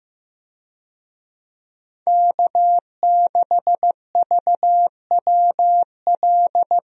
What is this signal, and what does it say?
Morse code, heard as a keyed tone, about 15 wpm, 700 Hz. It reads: K6VWL
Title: USA Call Signs - 5 Characters - Repeated 2x - 15wpm